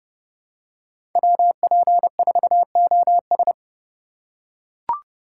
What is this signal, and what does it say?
Morse code, heard as a keyed tone, about 30 wpm, 700 Hz. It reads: WP4OS
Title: USA Call Signs - 5 Characters - 2x Character Spacing - 30wpm